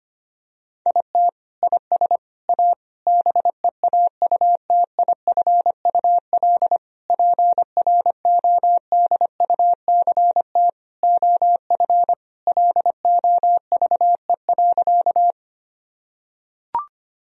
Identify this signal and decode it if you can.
Morse code, heard as a keyed tone, about 25 wpm, 700 Hz. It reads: IT IS A BEAUTIFUL PRODUCT OF LOVE.
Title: Sentences from Top 1000 Words - Review Part 2 - 25wpm